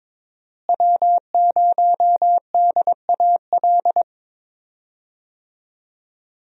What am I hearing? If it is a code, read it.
Morse code, heard as a keyed tone, about 22 wpm, 700 Hz. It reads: W0DAL
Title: USA Call Signs - 5 Characters - Repeated 3x - 22wpm